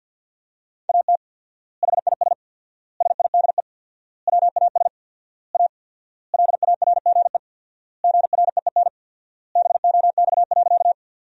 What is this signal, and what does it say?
Morse code, heard as a keyed tone, about 52 wpm, 700 Hz. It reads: AT HIS SIDE WAS A LARGE GREEN BOX.
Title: 4x Word-Spacing - Sentences from Top 500 Words - 50wpm